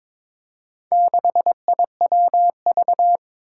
Morse code, heard as a keyed tone, about 22 wpm, 700 Hz. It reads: 6IWV
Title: Four Challenging Letter-Number Combinations 22wpm